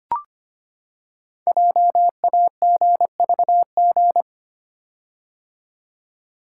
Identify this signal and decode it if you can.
Morse code, heard as a keyed tone, about 25 wpm, 700 Hz. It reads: JAGVG
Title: Code Groups - 5 Letters - 25wpm